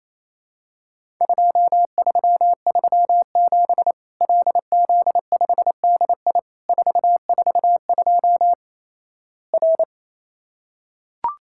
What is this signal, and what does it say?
Morse code, heard as a keyed tone, about 28 wpm, 700 Hz. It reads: 2337 LZ5DI 442 R
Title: Worked All Europe DX Contest - QTC Exchange - 28wpm